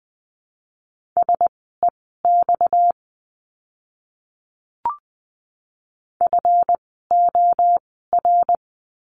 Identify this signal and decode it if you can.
Morse code, heard as a keyed tone, about 20 wpm, spaced as for 15 wpm, 700 Hz. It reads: SEX FOR
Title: 230 Lesson 37f - Letter x - Words 20wpm